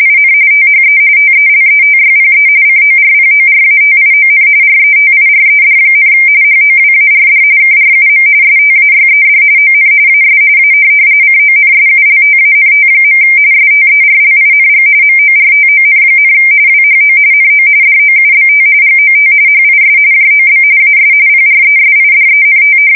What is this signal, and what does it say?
RTTY, about 45.45 baud. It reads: IN THE TELETYPE CORPORATION RETIRED, HOWARD KRUM STAYED ON AS A CONSULTANT. STERLING MORTON, WHO NO DOUBT GOT HIS JOB AS PRESIDENT
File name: fsk170 / teletype-corp